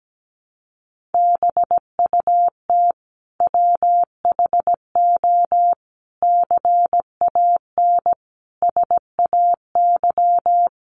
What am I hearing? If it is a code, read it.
Morse code, heard as a keyed tone, about 17 wpm, 700 Hz. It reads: BUT WHO CAN SAY
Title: Sentences from Top 1500 Words 17wpm